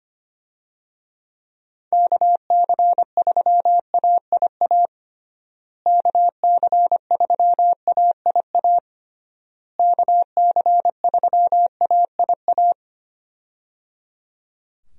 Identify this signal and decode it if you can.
Morse code, heard as a keyed tone, about 25 wpm, 700 Hz. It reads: KC3AIA KC3AIA KC3AIA
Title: USA Call Signs - 6 Characters - Repeated 3x - 25wpm